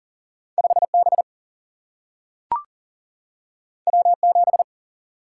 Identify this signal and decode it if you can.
Morse code, heard as a keyed tone, about 40 wpm, 700 Hz. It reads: 5B W7
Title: Single-Letter Single-Number 40wpm